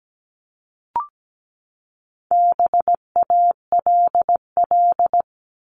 Morse code, heard as a keyed tone, about 17 wpm, 700 Hz. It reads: BALL
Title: Top 751-1000 Words 17wpm